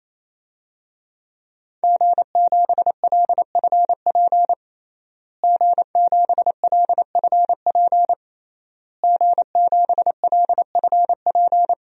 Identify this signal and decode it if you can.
Morse code, heard as a keyed tone, about 28 wpm, 700 Hz. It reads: G7LFP G7LFP G7LFP
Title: DXCC-100 Call Signs - 5 Characters - Repeated 3x - 28wpm